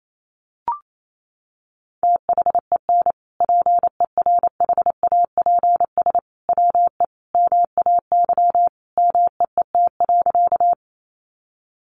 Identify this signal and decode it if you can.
Morse code, heard as a keyed tone, about 28 wpm, 700 Hz. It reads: THEN PERHAPS WE MAY MEET.
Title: Sentences from Top 1100 Words 28wpm